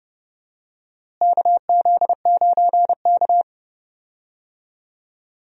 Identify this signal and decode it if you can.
Morse code, heard as a keyed tone, about 32 wpm, 700 Hz. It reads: KZ9K
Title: USA Call Signs - 4 Characters - Repeated 3x - 30wpm